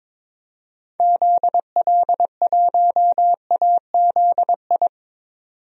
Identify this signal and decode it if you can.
Morse code, heard as a keyed tone, about 22 wpm, 700 Hz. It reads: ZL1AZI
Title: DXCC-100 Call Signs - 6 Characters - 22wpm